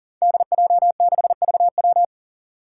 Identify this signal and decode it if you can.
Morse code, heard as a keyed tone, about 42 wpm, 700 Hz. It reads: DJ6VW
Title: Five Challenging Letter-Number Combinations 40wpm